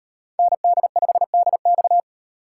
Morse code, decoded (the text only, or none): ND5DX